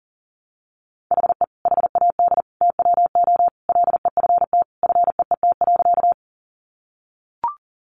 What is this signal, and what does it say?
Morse code, heard as a keyed tone, about 40 wpm, 700 Hz. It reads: HE HAD TWO LEFT FEET.
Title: Sentences from Top 300 Words 40wpm